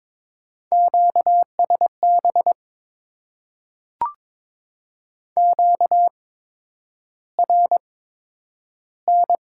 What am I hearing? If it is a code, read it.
Morse code, heard as a keyed tone, about 22 wpm, spaced as for 9 wpm, 700 Hz. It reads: QSB QRN
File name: Top 90 QSO Elements - 8x Character Spacing - 22wpm